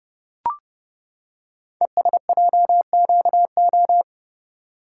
Morse code, heard as a keyed tone, about 30 wpm, 700 Hz. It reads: ESJQO
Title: Code Groups - 5 Letters - 30wpm